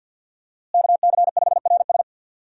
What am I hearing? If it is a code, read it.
Morse code, heard as a keyed tone, about 52 wpm, 700 Hz. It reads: KC5RS
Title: USA Call Signs - 5 Characters - Repeated 2x - 50wpm